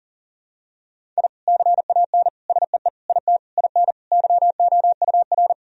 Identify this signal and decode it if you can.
Morse code, heard as a keyed tone, about 40 wpm, 700 Hz. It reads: I CAN SEE IT IN YOUR
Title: Sentences from Top 1000 Words - Review Part 1 - 40wpm